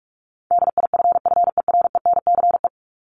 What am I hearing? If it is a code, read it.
Morse code, heard as a keyed tone, about 44 wpm, 700 Hz. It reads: DIFFERENCE
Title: Top 751-1000 Words 45wpm